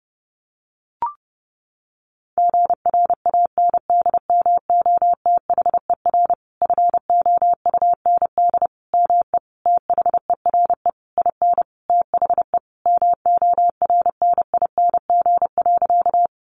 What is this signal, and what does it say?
Morse code, heard as a keyed tone, about 30 wpm, 700 Hz. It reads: GRANDMOTHER FOUND ME THERE IN THE MORNING.
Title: Sentences from Top 1200 Words 30wpm